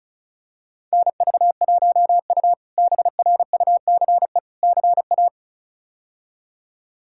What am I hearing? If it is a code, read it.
Morse code, heard as a keyed tone, about 36 wpm, 700 Hz. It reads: NV1U BRUCE CA
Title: K1USN SST QSO Practice - 35wpm